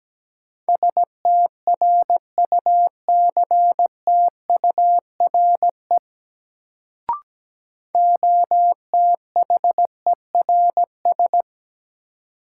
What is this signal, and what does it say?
Morse code, heard as a keyed tone, about 17 wpm, 700 Hz. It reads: STRUCTURE OTHERS
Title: Sentences from Top 1500 Words - New Word Drill Alt - 17wpm